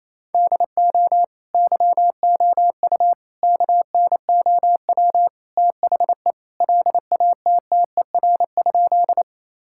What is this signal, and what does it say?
Morse code, heard as a keyed tone, about 27 wpm, 700 Hz. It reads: DO YOU KNOW THE LATTER?
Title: Sentences from Top 1600 Words 28wpm